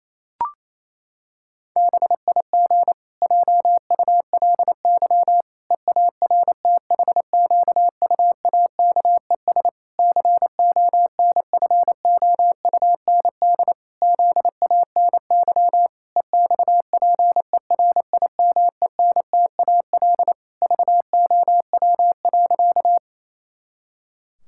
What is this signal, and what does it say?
Morse code, heard as a keyed tone, about 27 wpm, 700 Hz. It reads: BIG JULY EARTHQUAKES CONFOUND ZANY EXPERIMENTAL VOW.
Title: Pangram Challenge v2 - 28wpm